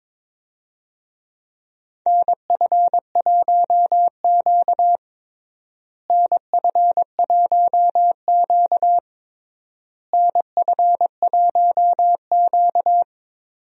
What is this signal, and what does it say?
Morse code, heard as a keyed tone, about 22 wpm, 700 Hz. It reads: NF1Q NF1Q NF1Q
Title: USA Call Signs - 4 Characters - Repeated 3x - 22wpm